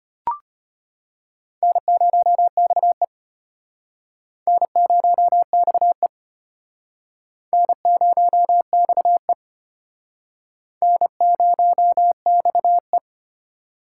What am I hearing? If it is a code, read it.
Morse code, heard as a keyed tone, about 34 wpm, 700 Hz. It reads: N0XE N0XE N0XE N0XE